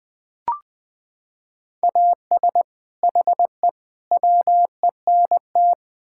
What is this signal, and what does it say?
Morse code, heard as a keyed tone, about 21 wpm, 700 Hz. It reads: AS HE WENT
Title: Sentences from Top 1600 Words 20wpm